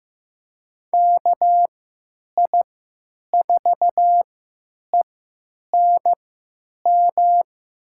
Morse code, heard as a keyed tone, about 15 wpm, spaced as for 9 wpm, 700 Hz. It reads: KI4ENM
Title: USA Call Signs - 6 Characters - 3x Character Spacing - 15wpm